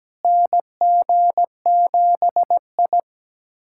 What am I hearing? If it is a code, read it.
Morse code, heard as a keyed tone, about 17 wpm, 700 Hz. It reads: NG7I
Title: USA Callsign DE Callsign - 4 Characters - 17wpm